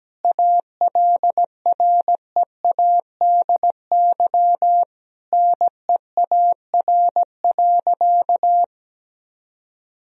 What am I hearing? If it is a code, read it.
Morse code, heard as a keyed tone, about 17 wpm, 700 Hz. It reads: ALREADY NEAR.
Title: Sentences from Top 1000 Words - Review Part 1 - 17wpm